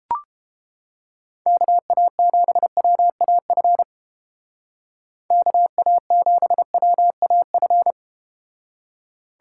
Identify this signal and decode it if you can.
Morse code, heard as a keyed tone, about 30 wpm, 700 Hz. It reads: KA7WAF KA7WAF